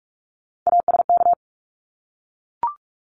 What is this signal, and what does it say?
Morse code, heard as a keyed tone, about 44 wpm, 700 Hz. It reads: ASK
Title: Top 300 Words Repeated 45wpm